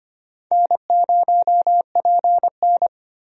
Morse code, heard as a keyed tone, about 25 wpm, 700 Hz. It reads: N0PN
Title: USA Call Signs - 4 Characters - 25wpm